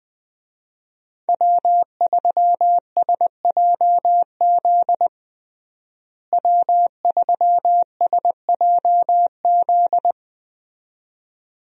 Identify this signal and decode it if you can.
Morse code, heard as a keyed tone, about 20 wpm, 700 Hz. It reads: W3SJZ W3SJZ
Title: USA Call Signs - 5 Characters - Repeated 2x - 20wpm